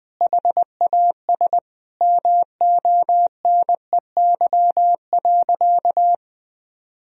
Morse code, heard as a keyed tone, about 21 wpm, 700 Hz. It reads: HAS MONEY.
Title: Sentences from Top 1000 Words - Review Part 3 - 20wpm